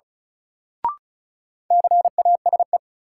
Morse code, heard as a keyed tone, about 34 wpm, 700 Hz. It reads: CASE
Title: Sets of 2 Words - New Word Drill - 35wpm